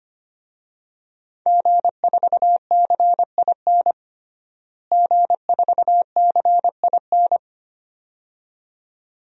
Morse code, heard as a keyed tone, about 25 wpm, 700 Hz. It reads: G4CIN G4CIN
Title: DXCC-100 Call Signs - 5 Characters - Repeated 2x - 25wpm